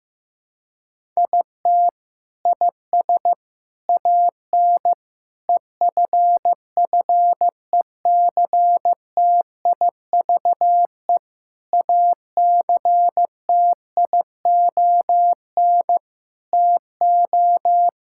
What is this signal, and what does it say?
Morse code, heard as a keyed tone, about 15 wpm, 700 Hz. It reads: IT IS AN EFFECTIVE ACTION TO
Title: Sentences from Top 1000 Words - Review Part 6 - 15wpm